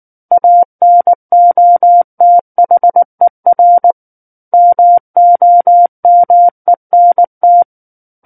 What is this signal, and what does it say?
Morse code, heard as a keyed tone, about 19 wpm, 700 Hz. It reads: ANOTHER MOMENT
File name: Book One / 10-Chapter 10 - 19 WPM